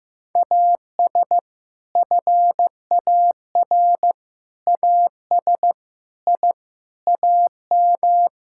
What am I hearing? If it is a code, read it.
Morse code, heard as a keyed tone, about 15 wpm, 700 Hz. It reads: AS FAR AS I AM
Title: Sentences from Top 1000 Words - Review Part 6 - 15wpm